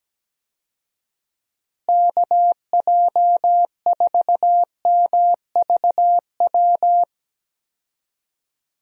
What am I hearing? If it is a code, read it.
Morse code, heard as a keyed tone, about 17 wpm, 700 Hz. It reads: KJ4MVW
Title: USA Call Signs - 6 Characters - Repeated 3x - 17wpm